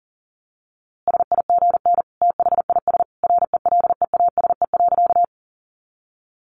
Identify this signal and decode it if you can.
Morse code, heard as a keyed tone, about 37 wpm, 700 Hz. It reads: SIGN THIS RELEASE.
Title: A Sound Of Thunder 40wpm